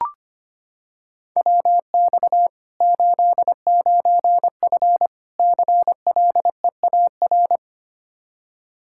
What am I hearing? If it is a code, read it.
Morse code, heard as a keyed tone, about 25 wpm, 700 Hz. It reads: WX 89F CLEAR
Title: Practice QSO - Exchange 2 - Info - 25wpm